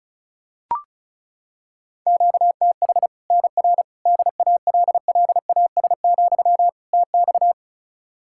Morse code, heard as a keyed tone, about 36 wpm, 700 Hz. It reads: QTH NR DALLAS, TX
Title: Practice QSO - Exchange 1 - Beginning - 35wpm